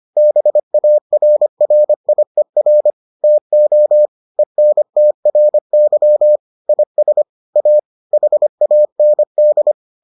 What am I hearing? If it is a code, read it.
Morse code, heard as a keyed tone, about 25 wpm, 600 Hz. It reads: BARRIER TO ENTRY IS A HAND